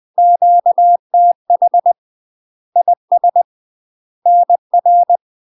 Morse code, heard as a keyed tone, about 20 wpm, 700 Hz. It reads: QTH IS NR
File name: UK MASTER QSOs TRAINING / 20WPM / UK MASTER QSOs 0063